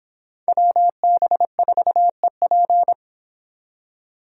Morse code, heard as a keyed tone, about 29 wpm, 700 Hz. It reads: WB4EP